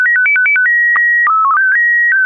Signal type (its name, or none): SSTV (slow-scan television)